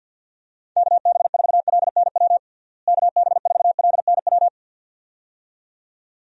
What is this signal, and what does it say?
Morse code, heard as a keyed tone, about 52 wpm, 700 Hz. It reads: KB4LNW KB4LNW
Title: USA Call Signs - 6 Characters - Repeated 2x - 50wpm